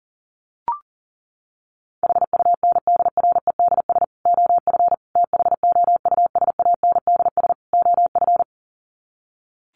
Morse code, heard as a keyed tone, about 40 wpm, 700 Hz. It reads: HUNDREDS OF THOUSANDS OF
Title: Sets of 4 Words 40wpm